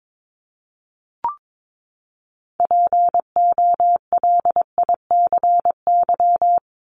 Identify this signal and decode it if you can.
Morse code, heard as a keyed tone, about 22 wpm, 700 Hz. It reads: POLICY
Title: Top 501-750 Words 22wpm